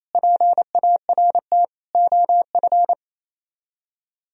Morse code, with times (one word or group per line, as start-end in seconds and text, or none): 0.1-1.6 PART
1.9-2.9 OF